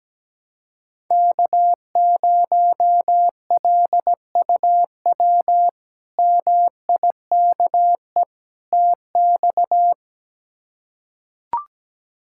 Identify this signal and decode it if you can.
Morse code, heard as a keyed tone, about 17 wpm, 700 Hz. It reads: K0LUW MIKE TX
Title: K1USN SST QSO Practice - 17wpm